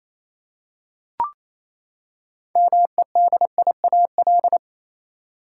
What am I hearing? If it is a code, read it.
Morse code, heard as a keyed tone, about 28 wpm, 700 Hz. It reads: MEDIAL
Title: Practice QSO - Exchange 3 - More Info - 28wpm